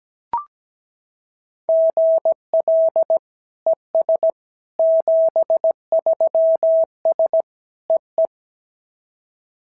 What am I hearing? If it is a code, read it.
Morse code, heard as a keyed tone, about 17 wpm, 650 Hz. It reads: GL ES 73S EE
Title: SOTA QSO Practice 17wpm